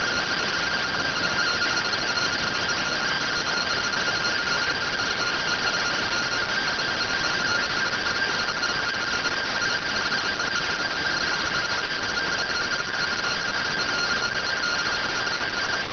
Two-tone teleprinter signal, about 170 Hz shift.